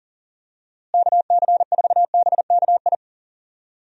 Morse code, heard as a keyed tone, about 40 wpm, 700 Hz. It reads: KC4BKI
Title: USA Call Signs - 6 Characters - Repeated 3x - 40wpm